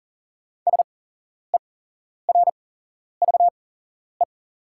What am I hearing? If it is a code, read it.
Morse code, heard as a keyed tone, about 40 wpm, spaced as for 11 wpm, 690 Hz. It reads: SERVE
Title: 8x Character Spacing - Top 500 Words - 40wpm